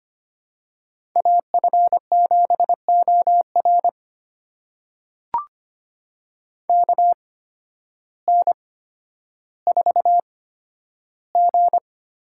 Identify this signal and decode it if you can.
Morse code, heard as a keyed tone, about 25 wpm, spaced as for 12 wpm, 700 Hz. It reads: AF7OR KN4G
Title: USA Call Signs - 5 Characters - 8x Character Spacing - 25wpm